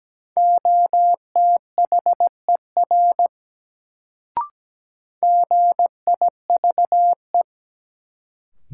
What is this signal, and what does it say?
Morse code, heard as a keyed tone, about 17 wpm, 700 Hz. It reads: OTHER GIVE
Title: Sets of 2 Words - New Word Drill - 17wpm